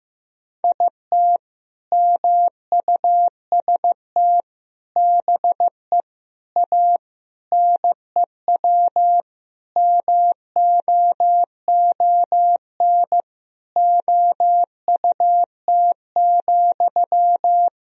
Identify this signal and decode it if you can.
Morse code, heard as a keyed tone, about 16 wpm, 700 Hz. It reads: IT MUST BE A NEW MOON OUT,
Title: Sentences from Top 500 Words and Abbreviations 15wpm